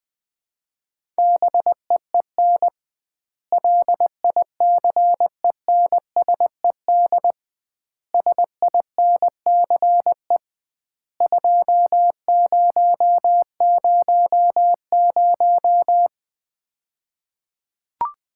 Morse code, heard as a keyed tone, about 20 wpm, 700 Hz. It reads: BEEN LICENSED SINCE 2000
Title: Common QSO Phrases - 2x Word-Spacing - 20wpm